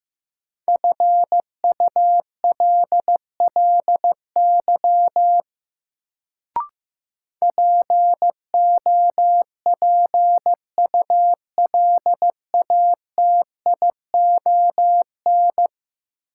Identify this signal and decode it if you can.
Morse code, heard as a keyed tone, about 16 wpm, 700 Hz. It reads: FULLY POPULATION
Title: Sentences from Top 1300 Words - New Word Drill Alt - 15wpm